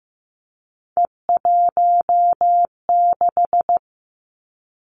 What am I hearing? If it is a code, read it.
Morse code, heard as a keyed tone, about 15 wpm, 700 Hz. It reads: E16
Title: Three Challenging Letter-Number Combinations 15wpm